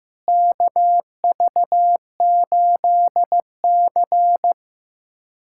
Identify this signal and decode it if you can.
Morse code, heard as a keyed tone, about 15 wpm, 700 Hz. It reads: KV8C
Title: USA Callsign DE Callsign - 4 Characters - 15wpm